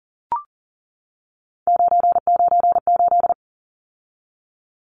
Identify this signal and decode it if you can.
Morse code, heard as a keyed tone, about 40 wpm, 700 Hz. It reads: 998
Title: Numbers - 3-Digit 40wpm